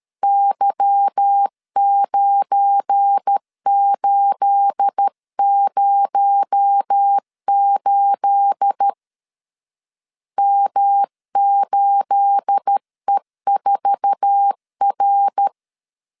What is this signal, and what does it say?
Morse code, heard as a keyed tone, about 13 wpm, 800 Hz. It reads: Y9808 M8E4R
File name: LEZIONI / 9 28 12wpm 2351